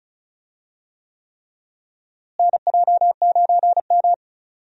Morse code, heard as a keyed tone, about 34 wpm, 700 Hz. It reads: NJ9M
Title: USA Call Signs - 4 Characters - Repeated 3x - 35wpm